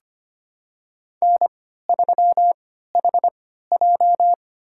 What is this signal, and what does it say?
Morse code, heard as a keyed tone, about 25 wpm, spaced as for 14 wpm, 700 Hz. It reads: N3HJ
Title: USA Call Signs - 4 Characters - 3x Character Spacing - 25wpm